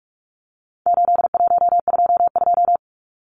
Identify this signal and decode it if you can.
Morse code, heard as a keyed tone, about 44 wpm, 700 Hz. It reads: 8122
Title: Numbers - 4-Digit 45wpm